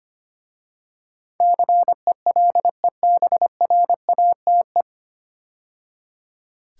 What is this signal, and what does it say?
Morse code, heard as a keyed tone, about 25 wpm, 700 Hz. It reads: CELEBRATE
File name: practice / 25wpm / 9 Letter Words - First 3 Letters Given - 25wpm